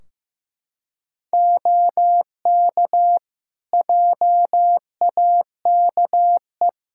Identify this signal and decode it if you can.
Morse code, heard as a keyed tone, about 15 wpm, 700 Hz. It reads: OK JAKE
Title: Practice QSO - Exchange 4 - Ending - 15wpm